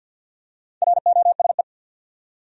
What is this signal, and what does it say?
Morse code, heard as a keyed tone, about 48 wpm, 700 Hz. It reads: ROSE
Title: USA Common Names - Max 5 Letters - 50wpm